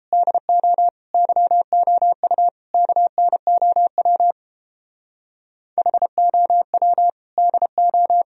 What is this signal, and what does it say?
Morse code, heard as a keyed tone, about 31 wpm, 700 Hz. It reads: DO YOU KNOW HOW DO